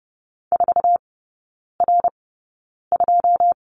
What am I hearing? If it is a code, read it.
Morse code, heard as a keyed tone, about 30 wpm, spaced as for 9 wpm, 700 Hz. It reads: 4R2
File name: course / 30:10wpm / 087 Lesson 15e - Number 5 - Three Characters 30wpm